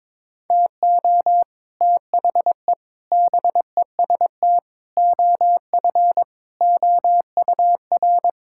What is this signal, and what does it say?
Morse code, heard as a keyed tone, about 22 wpm, 700 Hz. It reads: TO THE BEST OF OUR
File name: practice / 22wpm / Sentences with 8-Word Phrases - 22wpm